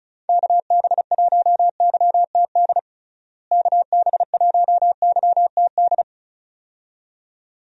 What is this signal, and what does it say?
Morse code, heard as a keyed tone, about 36 wpm, 700 Hz. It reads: KB1YTD KB1YTD
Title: USA Call Signs - 6 Characters - Repeated 2x - 35wpm